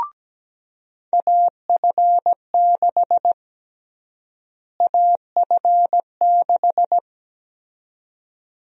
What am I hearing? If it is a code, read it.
Morse code, heard as a keyed tone, about 17 wpm, 700 Hz. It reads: AF6 AF6
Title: USA Call Sign Prefixes - Repeated 2x - 17wpm